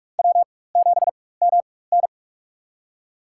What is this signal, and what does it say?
Morse code, heard as a keyed tone, about 46 wpm, spaced as for 21 wpm, 700 Hz. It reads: W7MN
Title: USA Call Signs - 4x Character Spacing - 45wpm